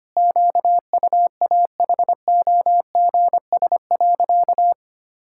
Morse code, heard as a keyed tone, about 25 wpm, 700 Hz. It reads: QUAHOGS.
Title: Pangram Challenge v2 - 25wpm